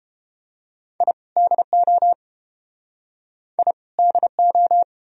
Morse code, heard as a keyed tone, about 31 wpm, 700 Hz. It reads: I DO I DO